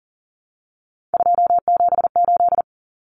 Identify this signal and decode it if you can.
Morse code, heard as a keyed tone, about 40 wpm, 700 Hz. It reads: 278
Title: Common Cut Numbers - 3-Digit 40wpm